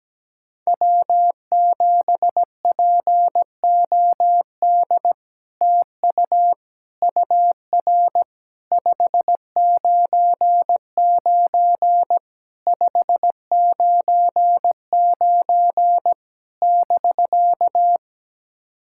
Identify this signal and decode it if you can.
Morse code, heard as a keyed tone, about 17 wpm, 700 Hz. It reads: W7POD TU UR 599 599 <BK>
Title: SOTA QSO Practice 17wpm